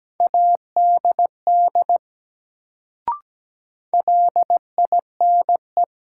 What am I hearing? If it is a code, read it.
Morse code, heard as a keyed tone, about 17 wpm, 700 Hz. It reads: ADD LINE
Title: Sentences from Top 1000 Words - Review Part 1 - New Word Drill - 17wpm